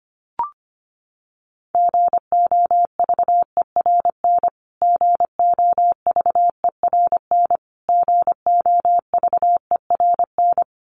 Morse code, heard as a keyed tone, about 25 wpm, 700 Hz. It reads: GOVERN GOVERN GOVERN
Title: Top 500 Words Repeated 25wpm